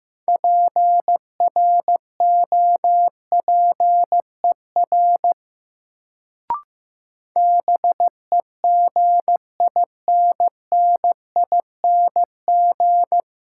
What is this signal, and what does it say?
Morse code, heard as a keyed tone, about 15 wpm, 700 Hz. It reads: PROPER BEGINNING
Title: Sentences from Top 800 Words - New Word Drill - 15wpm